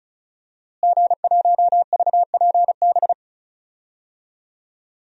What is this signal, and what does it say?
Morse code, heard as a keyed tone, about 34 wpm, 700 Hz. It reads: G1VPB
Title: DXCC-100 Call Signs - 5 Characters - Repeated 3x - 35wpm